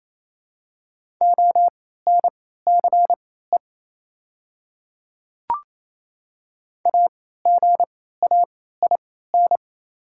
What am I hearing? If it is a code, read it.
Morse code, heard as a keyed tone, about 28 wpm, spaced as for 16 wpm, 690 Hz. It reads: ONCE AGAIN
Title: 3x Character Spacing - Top 300 Words - 28wpm